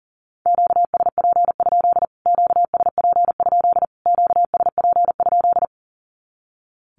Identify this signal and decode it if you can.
Morse code, heard as a keyed tone, about 37 wpm, 700 Hz. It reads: QSP? QSP? QSP?
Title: Q-Codes Repeated 3x 40wpm